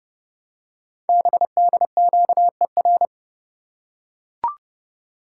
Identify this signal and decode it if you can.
Morse code, heard as a keyed tone, about 30 wpm, 700 Hz. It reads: BDQER
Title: Code Groups - 5 Letters - 30wpm